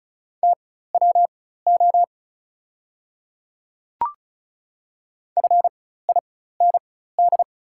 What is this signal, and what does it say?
Morse code, heard as a keyed tone, about 36 wpm, spaced as for 16 wpm, 700 Hz. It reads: TWO FIND
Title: 4x Character Spacing - Top 200 Words - 35wpm